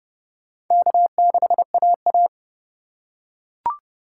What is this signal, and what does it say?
Morse code, heard as a keyed tone, about 25 wpm, 700 Hz. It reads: K6AA